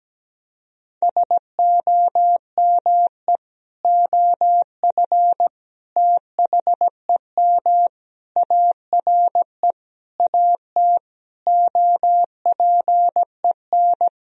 Morse code, heard as a keyed tone, about 17 wpm, 700 Hz. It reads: SOME OF THEM ARE AT OPEN
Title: Sentences from Top 1000 Words - Review Part 2 - 17wpm